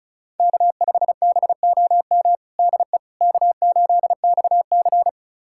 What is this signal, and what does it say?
Morse code, heard as a keyed tone, about 34 wpm, 700 Hz. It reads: K5BOM DE K8XC
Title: USA Callsign DE Callsign 35wpm